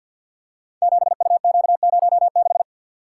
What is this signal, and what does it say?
Morse code, heard as a keyed tone, about 51 wpm, 700 Hz. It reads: 7UQ06
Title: Code Groups - 5 Letter:Numbers - 50wpm